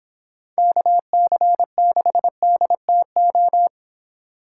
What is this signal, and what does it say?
Morse code, heard as a keyed tone, about 29 wpm, 700 Hz. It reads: KC6DTO